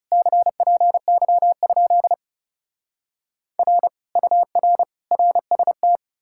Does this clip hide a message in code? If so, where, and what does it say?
Yes: Morse code from 0.1 to 6.0 s, heard as a keyed tone, about 40 wpm, 700 Hz. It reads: CPY? R UR RST